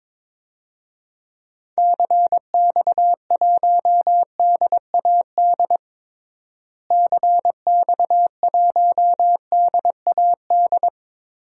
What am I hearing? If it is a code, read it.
Morse code, heard as a keyed tone, about 22 wpm, 700 Hz. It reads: CX1DAD CX1DAD
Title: DXCC-100 Call Signs - 6 Characters - Repeated 2x - 22wpm